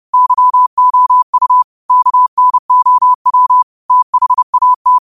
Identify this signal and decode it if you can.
Morse code, heard as a keyed tone, about 30 wpm, 1000 Hz. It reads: YOU KNOW THAT